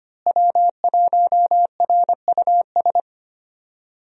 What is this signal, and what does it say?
Morse code, heard as a keyed tone, about 25 wpm, 700 Hz. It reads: W1RUS
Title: USA Call Signs - 5 Characters - Repeated 2x - 25wpm